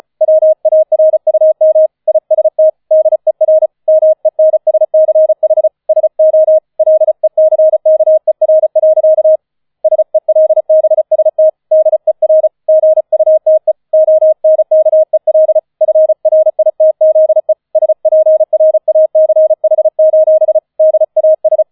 Morse code, heard as a keyed tone, about 34 wpm, 630 Hz. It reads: WARUM IST DER MENSCH SO LECKER. SELBST DER GUTE ONKEL FRITZE SPRACH: DAS